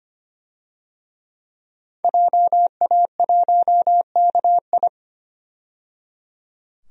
Morse code, heard as a keyed tone, about 25 wpm, 700 Hz. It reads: JA1KI
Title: DXCC-100 Call Signs - 5 Characters - 25wpm